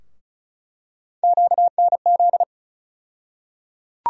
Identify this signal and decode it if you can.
Morse code, heard as a keyed tone, about 36 wpm, 700 Hz. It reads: QNZ
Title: USA Call Sign Postfixes - 3 Letters 35wpm